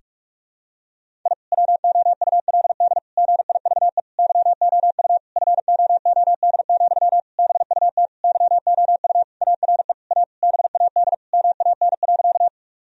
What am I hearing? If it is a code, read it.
Morse code, heard as a keyed tone, about 46 wpm, 700 Hz. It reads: I WOULD GIVE YOU FOOD, BUT YOU ARE A BAD MAN.
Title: Sentences from Top 1000 Words - Review Part 1 - 45wpm